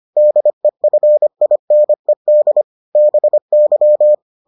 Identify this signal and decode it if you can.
Morse code, heard as a keyed tone, about 25 wpm, 600 Hz. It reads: DEFINED BY